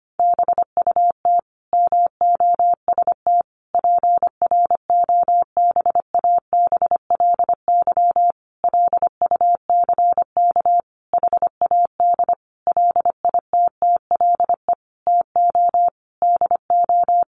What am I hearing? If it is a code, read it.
Morse code, heard as a keyed tone, about 25 wpm, 700 Hz. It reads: BUT MOST PROBABLY LUCK HAD LITTLE TO DO
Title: Sentences from Top 1500 Words 25wpm